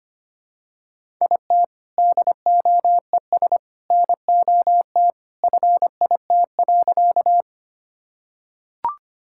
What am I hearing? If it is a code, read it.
Morse code, heard as a keyed tone, about 25 wpm, 700 Hz. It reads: IT DOES NOT FIT.
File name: practice / 25wpm / Sentences with 2-Word Phrases - 25wpm